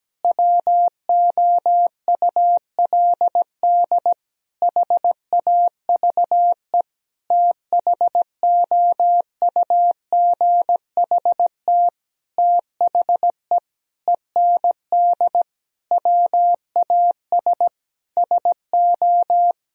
Morse code, heard as a keyed tone, about 17 wpm, 700 Hz. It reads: WOULD HAVE THOUGHT THE END WAS SO
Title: Sentences from Top 1000 Words - Review Part 1 - 17wpm